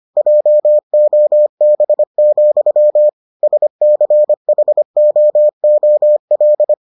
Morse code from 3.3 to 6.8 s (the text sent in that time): SCHOOL